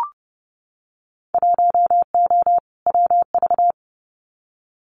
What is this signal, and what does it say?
Morse code, heard as a keyed tone, about 30 wpm, 700 Hz. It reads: 1O WV
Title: Winter Field Day Exchanges Encore 30wpm